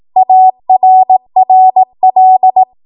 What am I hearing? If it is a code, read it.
Morse code, heard as a keyed tone, about 18 wpm, 750 Hz. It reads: ARRL
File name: Morse / 18WPM